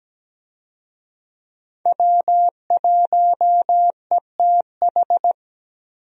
Morse code, heard as a keyed tone, about 17 wpm, 700 Hz. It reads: W1ETH
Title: USA Call Signs - 5 Characters - Repeated 2x - 17wpm